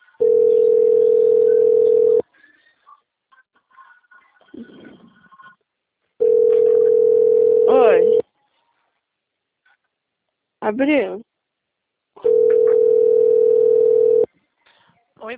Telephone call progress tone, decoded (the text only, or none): ringback tone